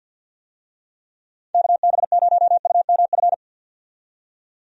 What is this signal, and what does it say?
Morse code, heard as a keyed tone, about 49 wpm, 700 Hz. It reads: KB0UMF
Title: USA Call Signs - 6 Characters - 50wpm